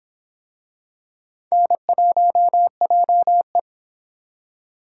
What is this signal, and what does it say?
Morse code, heard as a keyed tone, about 24 wpm, 700 Hz. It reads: N1JE